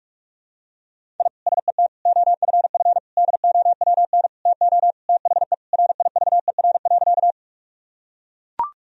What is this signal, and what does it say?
Morse code, heard as a keyed tone, about 41 wpm, 700 Hz. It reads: I SET OFF DOWN TO THE RIVER.
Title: Sentences from Top 1000 Words - Review Part 1 - 45wpm